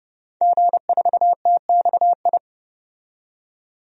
Morse code, heard as a keyed tone, about 31 wpm, 700 Hz. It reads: G4TXI
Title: DXCC-100 Call Signs - 5 Characters - 30wpm